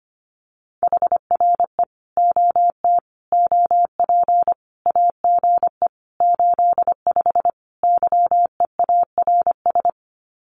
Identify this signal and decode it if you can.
Morse code, heard as a keyed tone, about 25 wpm, 700 Hz. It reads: HRE OT OP AGE 85 YEARS
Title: Common QSO Phrases 25wpm